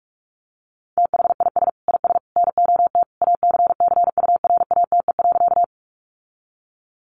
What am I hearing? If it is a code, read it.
Morse code, heard as a keyed tone, about 44 wpm, 700 Hz. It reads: THIS IS NOT ACCURATE.
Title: Sentences from Top 1200 Words 45wpm